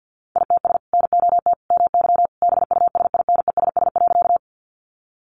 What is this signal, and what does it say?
Morse code, heard as a keyed tone, about 48 wpm, 700 Hz. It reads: ITS NOT MY BUSINESS.
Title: Sentences from Top 600 Words 50wpm